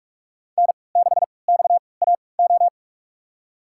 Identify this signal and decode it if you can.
Morse code, heard as a keyed tone, about 46 wpm, spaced as for 26 wpm, 700 Hz. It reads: N6XAO
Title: USA Call Signs - 3x Character Spacing - 45wpm